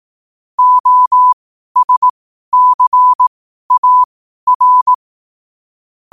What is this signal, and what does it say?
Morse code, heard as a keyed tone, about 18 wpm, spaced as for 12 wpm, 1000 Hz. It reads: OSCAR
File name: qotdfiles / 12wpm